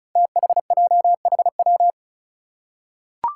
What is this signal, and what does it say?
Morse code, heard as a keyed tone, about 36 wpm, 700 Hz. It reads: THJHW
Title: Five Challenging Letter-Number Combinations 35wpm